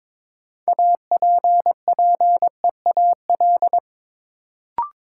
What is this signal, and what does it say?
Morse code, heard as a keyed tone, about 22 wpm, 700 Hz. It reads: APPEAL
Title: Sentences from Top 1300 Words - New Word Drill - 22wpm